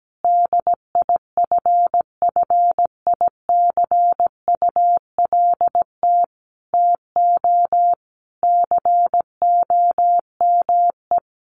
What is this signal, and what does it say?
Morse code, heard as a keyed tone, about 17 wpm, 700 Hz. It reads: DIFFICULT TO COME
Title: Sentences from Top 1500 Words 17wpm